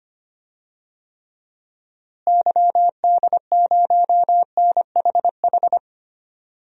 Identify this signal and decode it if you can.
Morse code, heard as a keyed tone, about 25 wpm, 700 Hz. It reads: YD0NHH
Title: DXCC-100 Call Signs - 6 Characters - Repeated 2x - 25wpm